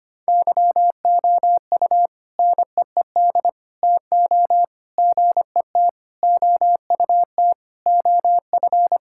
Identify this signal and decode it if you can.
Morse code, heard as a keyed tone, about 25 wpm, 700 Hz. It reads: YOU NEED TO GET OUT OF
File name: practice / 25wpm / Sentences with 3-Word Phrases - 25wpm